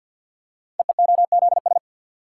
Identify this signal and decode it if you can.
Morse code, heard as a keyed tone, about 47 wpm, 700 Hz. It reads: EEOZS
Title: Code Groups - 5 Letters - 50wpm